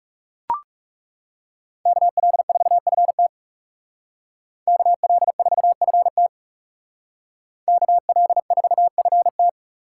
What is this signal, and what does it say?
Morse code, heard as a keyed tone, about 35 wpm, 700 Hz. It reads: KL4FT KL4FT KL4FT
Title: Speed Racing - USA Call Signs 30wpm